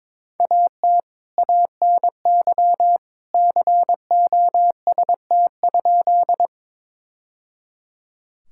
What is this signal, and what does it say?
Morse code, heard as a keyed tone, about 22 wpm, 700 Hz. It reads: AT ANY COST?
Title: Sentences from Top 1000 Words - Review Part 5 - 22wpm